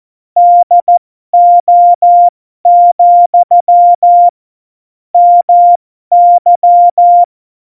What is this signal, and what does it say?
Morse code, heard as a keyed tone, about 14 wpm, spaced as for 12 wpm, 700 Hz. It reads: DO, MY
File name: Morsekurs / Witze